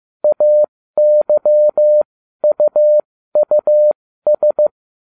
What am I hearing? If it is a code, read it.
Morse code, heard as a keyed tone, about 15 wpm, spaced as for 13 wpm, 600 Hz. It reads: AYUUS